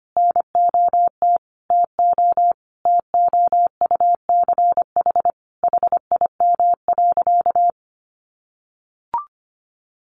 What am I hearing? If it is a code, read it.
Morse code, heard as a keyed tone, about 25 wpm, 700 Hz. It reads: NOT TO TOUCH HIM.
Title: Sentences from Top 1200 Words 25wpm